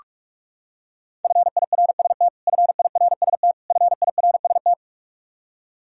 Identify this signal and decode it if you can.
Morse code, heard as a keyed tone, about 46 wpm, 700 Hz. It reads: FIRST FIRST FIRST
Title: Top 100 Words Repeated 45wpm